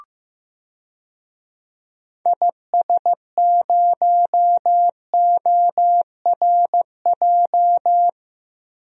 Morse code, heard as a keyed tone, about 15 wpm, 700 Hz. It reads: IS0ORJ
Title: DXCC-100 Call Signs - 6 Characters - Repeated 2x - 15wpm